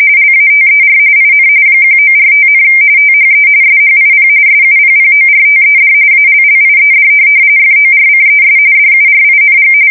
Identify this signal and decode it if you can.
RTTY, about 45.45 baud. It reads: LATER IN 1945, THE SHIP SUPPORTED ALLIED FORCES DURING
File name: fsk170 / USS-Massachusetts